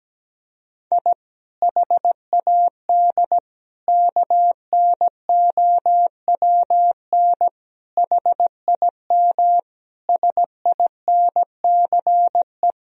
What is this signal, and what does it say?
Morse code, heard as a keyed tone, about 17 wpm, 700 Hz. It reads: I HAD KNOWN HIM SINCE
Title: Sentences from Top 1000 Words - Review Part 3 - 17wpm